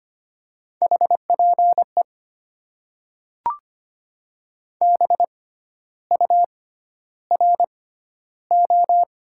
Morse code, heard as a keyed tone, about 25 wpm, spaced as for 12 wpm, 700 Hz. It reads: HPE BURO
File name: Top 90 QSO Elements - 6x Character Spacing - 25wpm